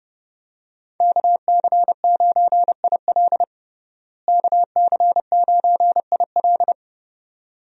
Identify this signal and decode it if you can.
Morse code, heard as a keyed tone, about 30 wpm, 700 Hz. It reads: KC9IL KC9IL
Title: USA Call Signs - 5 Characters - Repeated 2x - 30wpm